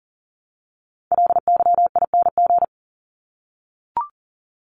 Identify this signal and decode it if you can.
Morse code, heard as a keyed tone, about 40 wpm, 700 Hz. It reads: LYING
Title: ING Suffix - Top 100 Words with Base Words 40wpm